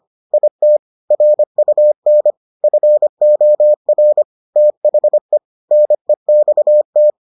Morse code, heard as a keyed tone, about 25 wpm, 600 Hz. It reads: IT RUN FOR THE NEXT